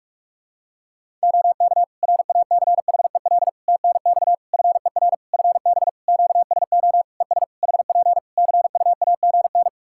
Morse code, heard as a keyed tone, about 46 wpm, 700 Hz. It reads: OK RACHEL TNX FER FB QSO ES HP CUAGN
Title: K1USN SST QSO Practice 45wpm